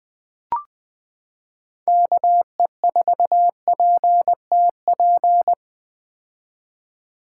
Morse code, heard as a keyed tone, about 20 wpm, 700 Hz. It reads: KE4PTP
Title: 2023 POTA Hunter Call Signs - 20wpm